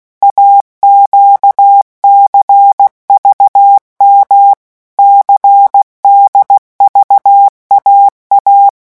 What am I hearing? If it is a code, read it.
Morse code, heard as a keyed tone, about 16 wpm, 800 Hz. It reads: AQCVM CDVAA
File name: morse mp3 / revision3-vitesse 15 mots